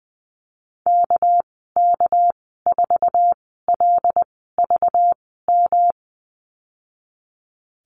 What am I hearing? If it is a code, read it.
Morse code, heard as a keyed tone, about 20 wpm, spaced as for 15 wpm, 700 Hz. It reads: KK4LVM
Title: 207 Lesson 33g - Letter j - Call Signs 20wpm